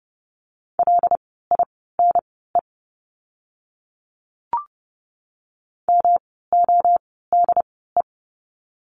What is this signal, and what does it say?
Morse code, heard as a keyed tone, about 30 wpm, spaced as for 17 wpm, 700 Hz. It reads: LINE MODE
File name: course / 30:10wpm / 108 Lesson 18f - Letter m - Words 30wpm